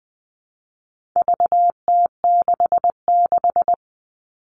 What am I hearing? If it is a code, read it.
Morse code, heard as a keyed tone, about 20 wpm, 700 Hz. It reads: VT66
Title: All Cut Numbers - 4-Digit 20wpm